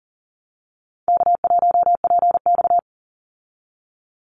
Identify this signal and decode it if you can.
Morse code, heard as a keyed tone, about 40 wpm, 700 Hz. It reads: K1PX
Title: Easy Call Signs Repeated 2x - 40wpm